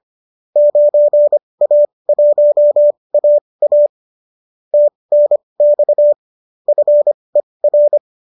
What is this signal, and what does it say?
Morse code, heard as a keyed tone, about 25 wpm, spaced as for 20 wpm, 590 Hz. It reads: 9A1AA TNX FER